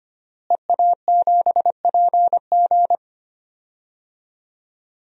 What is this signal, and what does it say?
Morse code, heard as a keyed tone, about 25 wpm, 700 Hz. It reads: EA7PG
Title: DXCC-100 Call Signs - 5 Characters - Repeated 3x - 25wpm